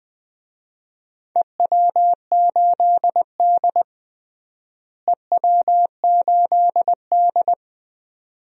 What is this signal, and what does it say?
Morse code, heard as a keyed tone, about 21 wpm, 700 Hz. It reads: EW8D EW8D
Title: DXCC-100 Call Signs - 4 Characters - Repeated 2x - 20wpm